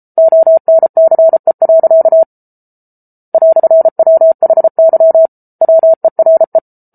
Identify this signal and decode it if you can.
Morse code, heard as a keyed tone, about 34 wpm, 650 Hz. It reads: ONCE. "WHY WERE